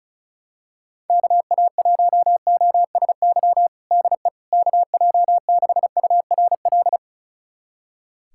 Morse code, heard as a keyed tone, about 36 wpm, 700 Hz. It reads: KA1OSY DE KJ6URL
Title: USA Callsign DE Callsign - 6 Characters - 35wpm